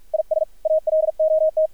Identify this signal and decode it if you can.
Morse code, heard as a keyed tone, about 53 wpm, 630 Hz. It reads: IS ALOT
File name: archived / CRSnet